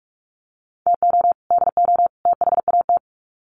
Morse code, heard as a keyed tone, about 44 wpm, 700 Hz. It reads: TO DO THAT
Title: Sets of 3 Words 45wpm